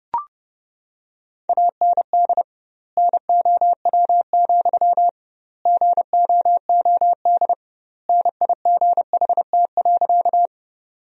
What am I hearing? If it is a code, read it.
Morse code, heard as a keyed tone, about 32 wpm, 700 Hz. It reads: AND NOW, GOOD NIGHT.
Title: 2x Word-Spacing - Sentences from Top 300 Words - 30wpm